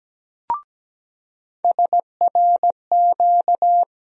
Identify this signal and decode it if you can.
Morse code, heard as a keyed tone, about 17 wpm, 700 Hz. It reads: SRQ
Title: USA Call Sign Postfixes - 3 Letters 17wpm